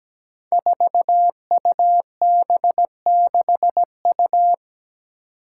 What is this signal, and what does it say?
Morse code, heard as a keyed tone, about 17 wpm, 700 Hz. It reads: 4UB6U
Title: Five Challenging Letter-Number Combinations 17wpm